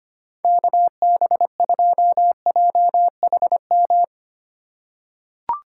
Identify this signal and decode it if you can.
Morse code, heard as a keyed tone, about 25 wpm, 690 Hz. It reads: KB2JHM
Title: USA Call Signs - 6 Characters - 2x Character Spacing - 25wpm